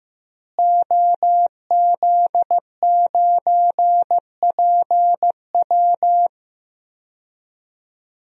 Morse code, heard as a keyed tone, about 16 wpm, 700 Hz. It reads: OZ9PW
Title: DXCC-100 Call Signs - 5 Characters - Repeated 2x - 15wpm